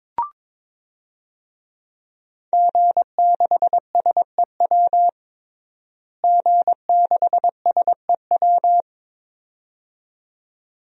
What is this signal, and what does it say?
Morse code, heard as a keyed tone, about 22 wpm, 700 Hz. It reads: G6SEW G6SEW
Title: DXCC-100 Call Signs - 5 Characters - Repeated 2x - 22wpm